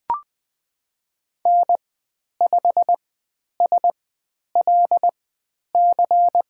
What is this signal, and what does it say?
Morse code, heard as a keyed tone, about 20 wpm, spaced as for 10 wpm, 700 Hz. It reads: N5SLC
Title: 102 Lesson 17g - Letter u - Call Signs 20wpm